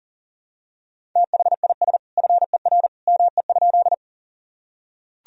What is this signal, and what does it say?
Morse code, heard as a keyed tone, about 40 wpm, 700 Hz. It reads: THIS FER ME?
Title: Sentences from Top 200 Words and Abbreviations 40wpm